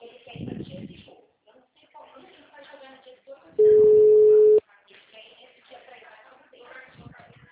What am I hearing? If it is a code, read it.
Ringback tone (Europe, 425 Hz), with speech.